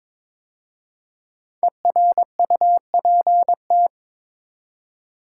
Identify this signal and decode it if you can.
Morse code, heard as a keyed tone, about 22 wpm, 690 Hz. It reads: ERUPT
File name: 5 Letter Words - First 3 Letters Given - 22wpm